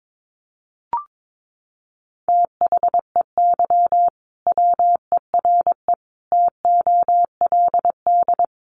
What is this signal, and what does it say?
Morse code, heard as a keyed tone, about 22 wpm, 700 Hz. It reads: THEY WERE TOLD
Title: Sentences from Top 1300 Words 22wpm